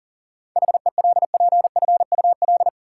Morse code, heard as a keyed tone, about 40 wpm, 700 Hz. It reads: HELPFUL
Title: Sentences from Top 1000 Words - Review Part 6 - New Word Drill - 40wpm